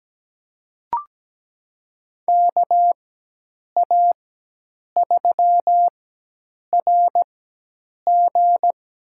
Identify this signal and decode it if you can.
Morse code, heard as a keyed tone, about 17 wpm, spaced as for 8 wpm, 700 Hz. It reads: KA3RG
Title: USA Call Signs - 4x Character Spacing - 17wpm